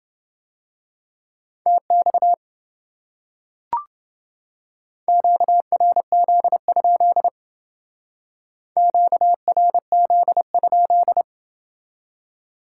Morse code, heard as a keyed tone, about 29 wpm, 700 Hz. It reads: TX QRZ? QRZ?